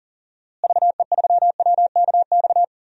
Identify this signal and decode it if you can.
Morse code, heard as a keyed tone, about 40 wpm, 700 Hz. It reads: VE3WKX
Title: DXCC-100 Call Signs - 6 Characters - Repeated 3x - 40wpm